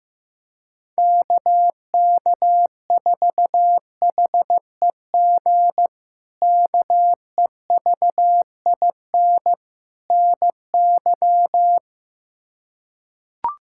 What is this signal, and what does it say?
Morse code, heard as a keyed tone, about 15 wpm, 700 Hz. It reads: KK4HEG KEVIN NY
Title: K1USN SST QSO Practice - 15wpm